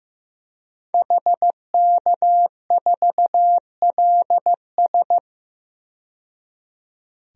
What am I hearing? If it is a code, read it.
Morse code, heard as a keyed tone, about 16 wpm, 700 Hz. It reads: HK4LS
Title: DXCC-100 Call Signs - 5 Characters - 15wpm